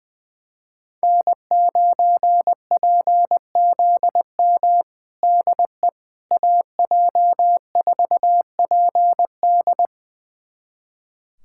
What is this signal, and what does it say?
Morse code, heard as a keyed tone, about 21 wpm, 700 Hz. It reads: N9PZM DE AJ4PD
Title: USA Callsign DE Callsign - 5 Characters - 20wpm